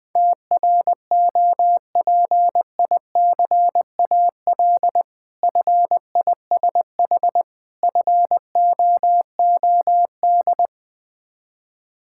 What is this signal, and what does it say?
Morse code, heard as a keyed tone, about 20 wpm, 690 Hz. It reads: TROPICAL FISH FOOD
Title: Difficult Words with Hints - 20wpm